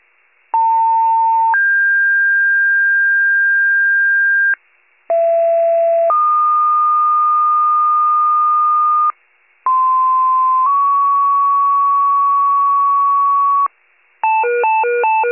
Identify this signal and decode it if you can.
Two-tone page, heard as a one-second tone, then a three-second tone, about 900 then 1600 Hz, about 670 then 1220 Hz, about 1040 then 1120 Hz. It is heard from 0.5 to 13.7 s.